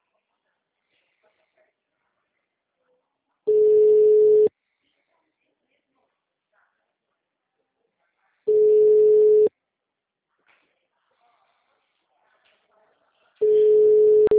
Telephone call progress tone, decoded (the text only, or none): ringback tone